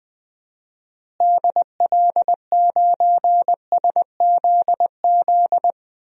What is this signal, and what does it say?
Morse code, heard as a keyed tone, about 20 wpm, 700 Hz. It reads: DL9SZZ
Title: DXCC-100 Call Signs - 6 Characters - Repeated 2x - 20wpm